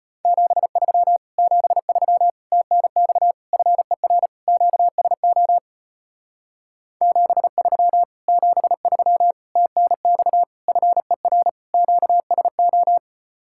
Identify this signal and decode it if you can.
Morse code, heard as a keyed tone, about 33 wpm, 690 Hz. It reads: 73 73 TNX FER QSO 73 73 TNX FER QSO